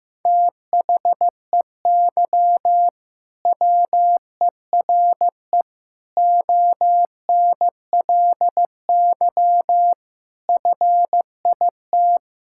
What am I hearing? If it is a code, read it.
Morse code, heard as a keyed tone, about 15 wpm, 700 Hz. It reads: THEY WERE ONLY FIT
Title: Sentences from Top 1000 Words - Review Part 5 - 15wpm